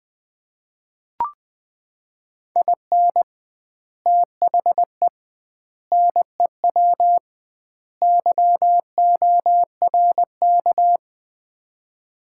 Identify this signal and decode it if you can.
Morse code, heard as a keyed tone, about 20 wpm, 700 Hz. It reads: IN THE NEW YORK
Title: 2x Word-Spacing - Sets of 4 Words - 20wpm